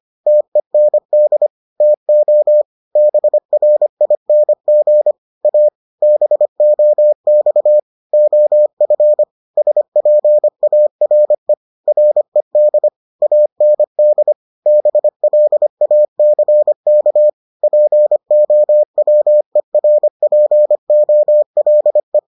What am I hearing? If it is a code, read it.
Morse code, heard as a keyed tone, about 25 wpm, 600 Hz. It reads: TEND TO BRING A BOX OF SPARE RED AND BLACK POWERPOLE